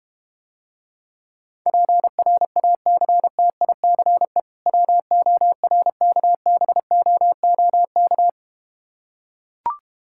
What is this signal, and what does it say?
Morse code, heard as a keyed tone, about 32 wpm, 700 Hz. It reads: PRACTICE WORKBOOK